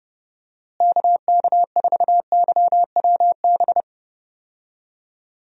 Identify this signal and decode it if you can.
Morse code, heard as a keyed tone, about 31 wpm, 700 Hz. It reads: KK4YWB
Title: USA Call Signs - 6 Characters - Repeated 3x - 30wpm